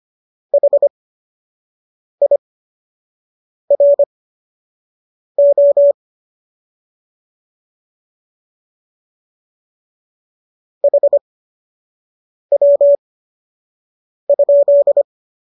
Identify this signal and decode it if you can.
Morse code, heard as a keyed tone, about 25 wpm, spaced as for 6 wpm, 570 Hz. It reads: HIRO HW?